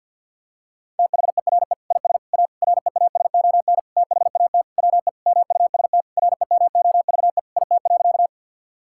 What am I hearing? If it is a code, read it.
Morse code, heard as a keyed tone, about 50 wpm, 700 Hz. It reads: THERE IS A REASON THAT WE MUST REMOVE IT.
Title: Sentences from Top 1000 Words - Review Part 5 - 50wpm